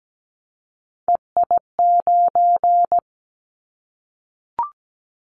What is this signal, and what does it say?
Morse code, heard as a keyed tone, about 17 wpm, 700 Hz. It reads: EI9
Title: Callsign Prefixes for Top 50 DXCC - 17wpm